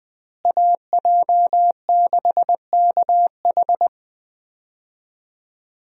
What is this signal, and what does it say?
Morse code, heard as a keyed tone, about 21 wpm, 700 Hz. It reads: AJ6KH
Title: USA Call Signs - 5 Characters - Repeated 3x - 20wpm